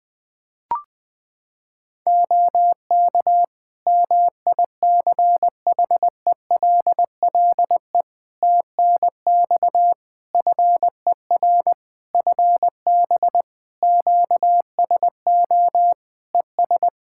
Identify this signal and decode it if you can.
Morse code, heard as a keyed tone, about 20 wpm, 700 Hz. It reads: OK MICHELLE TNX FER FB QSO ES
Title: Practice QSO - Exchange 4 - Ending - 20wpm